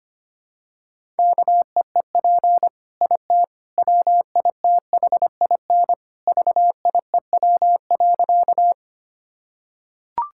Morse code, heard as a keyed tone, about 25 wpm, 700 Hz. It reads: KEEP IT WITHIN VIEW.
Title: Sentences from Top 1000 Words - Review Part 6 - 25wpm